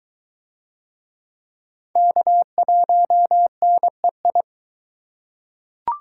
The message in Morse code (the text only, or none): K1NEI